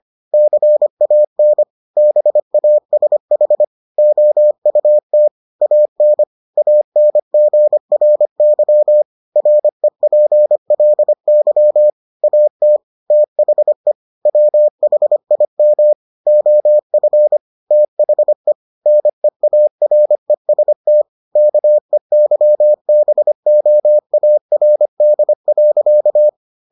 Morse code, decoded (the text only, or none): CAN BASH OUT AN ANGRY REPLY AT THE WHIM OF THE NEAREST KEYBOARD.